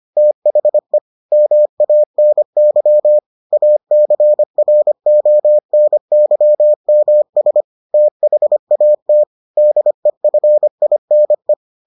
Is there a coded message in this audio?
Yes: Morse code from 0.2 to 11.5 s, heard as a keyed tone, about 25 wpm, 600 Hz. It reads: THE MANY ACRONYMS THAT DEFINE